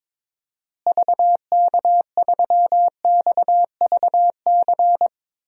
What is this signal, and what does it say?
Morse code, heard as a keyed tone, about 22 wpm, 700 Hz. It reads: VK3XVC
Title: DXCC-100 Call Signs - 6 Characters - Repeated 3x - 22wpm